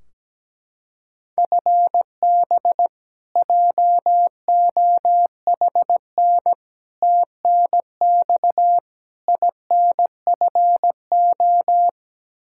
Morse code, heard as a keyed tone, about 17 wpm, 700 Hz. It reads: FB JOHN TNX INFO
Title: Practice QSO - Exchange 3 - More Info - 17wpm